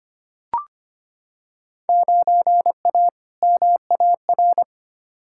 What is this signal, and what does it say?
Morse code, heard as a keyed tone, about 25 wpm, 700 Hz. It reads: 9A MAR
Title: Field Day Exchanges Encore - 25wpm